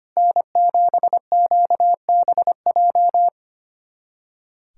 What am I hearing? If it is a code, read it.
Morse code, heard as a keyed tone, about 25 wpm, 700 Hz. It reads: N7QBJ
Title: USA Callsign DE Callsign - 5 Characters - 25wpm